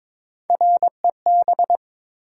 Morse code, heard as a keyed tone, about 22 wpm, 700 Hz. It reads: REB
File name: practice / 22wpm / Three Letter-Number Combinations - 22wpm